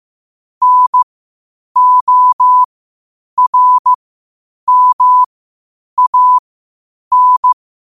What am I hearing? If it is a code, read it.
Morse code, heard as a keyed tone, about 15 wpm, spaced as for 8 wpm, 1000 Hz. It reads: NORMAN
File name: qotdfiles / 8wpm